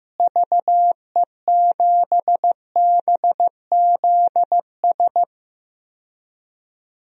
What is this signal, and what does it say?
Morse code, heard as a keyed tone, about 15 wpm, 700 Hz. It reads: VE7BZS
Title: DXCC-100 Call Signs - 6 Characters - Repeated 2x - 15wpm